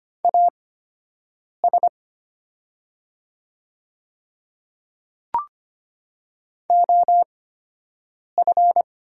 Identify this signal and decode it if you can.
Morse code, heard as a keyed tone, about 25 wpm, spaced as for 7 wpm, 700 Hz. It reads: AS OF
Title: 8x Character Spacing - Top 300 Words - 25wpm